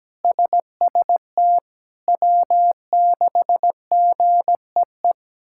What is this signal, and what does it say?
Morse code, heard as a keyed tone, about 17 wpm, 700 Hz. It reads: SST W6GEE
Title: K1USN SST QSO Practice - 17wpm